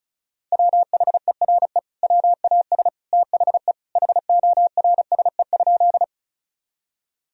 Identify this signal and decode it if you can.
Morse code, heard as a keyed tone, about 36 wpm, 700 Hz. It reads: WHERE WAS THE HORSE?
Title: Sentences from Top 1000 Words - Review Part 1 - 35wpm